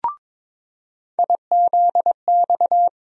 Morse code, heard as a keyed tone, about 22 wpm, 700 Hz. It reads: IZX
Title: USA Call Sign Postfixes - 3 Letters 22wpm